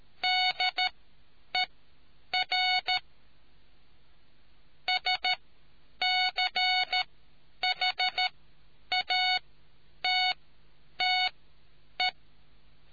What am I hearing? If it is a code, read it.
Morse code, heard as a keyed tone, about 13 wpm, spaced as for 8 wpm, 2200 Hz. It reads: DER SCHATTE